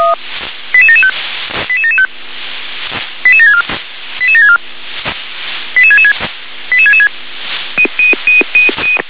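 Five-tone selective call (ZVEI), heard as quick runs of five tones, tones of about 70 ms. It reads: 88775 88775 88764 88764 88666 88666
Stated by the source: ZVEI five-tone selective call received on SDR.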